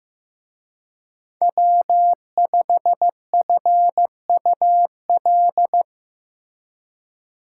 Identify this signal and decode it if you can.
Morse code, heard as a keyed tone, about 18 wpm, 700 Hz. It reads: W5FUL
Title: Speed Racing - USA 1x3 Call Signs 15wpm